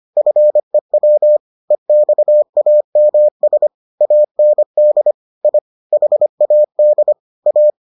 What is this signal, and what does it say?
Morse code, heard as a keyed tone, about 25 wpm, 600 Hz. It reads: FEW EXAMS AND I HAD A